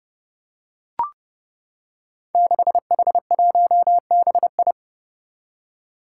Morse code, heard as a keyed tone, about 28 wpm, 700 Hz. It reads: 6H1BI
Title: Five Challenging Letter-Number Combinations 30wpm